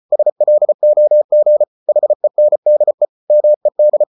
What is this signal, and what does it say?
Morse code, heard as a keyed tone, about 34 wpm, 600 Hz. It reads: SLOG HENDE MED
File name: Ferie QRQ 34wpm